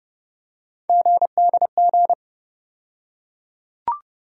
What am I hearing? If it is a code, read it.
Morse code, heard as a keyed tone, about 32 wpm, 700 Hz. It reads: GDG
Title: USA Call Sign Postfixes - 3 Letters 30wpm